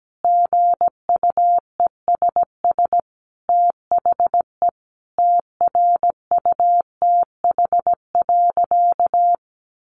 Morse code, heard as a keyed tone, about 17 wpm, 700 Hz. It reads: GUESS THE TRUTH.
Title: Sentences from Top 1500 Words 17wpm